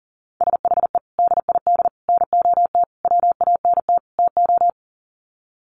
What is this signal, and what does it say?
Morse code, heard as a keyed tone, about 40 wpm, 700 Hz. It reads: SHE DID NOT WANT TO
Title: Sets of 5 Words 40wpm